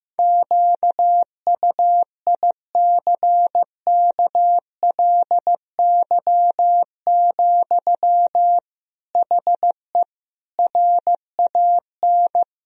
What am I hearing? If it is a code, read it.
Morse code, heard as a keyed tone, about 15 wpm, 700 Hz. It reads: QUICKLY, HE RAN
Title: Sentences from Top 1000 Words - Review Part 3 - 15wpm